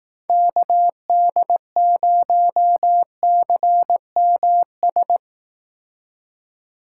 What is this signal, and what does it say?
Morse code, heard as a keyed tone, about 18 wpm, 700 Hz. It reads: KD0CMS
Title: Speed Racing - USA Call Signs 15wpm